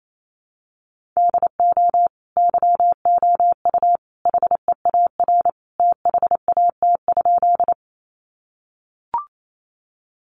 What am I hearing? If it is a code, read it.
Morse code, heard as a keyed tone, about 28 wpm, 700 Hz. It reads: DO YOU HEAR THAT?
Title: Sentences from Top 300 Words 28wpm